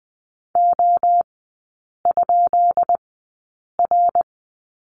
Morse code, heard as a keyed tone, about 20 wpm, spaced as for 8 wpm, 690 Hz. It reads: O?R
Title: 172 Lesson 28e - Number 9 - Three Characters 20wpm